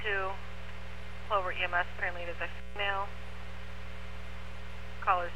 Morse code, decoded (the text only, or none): none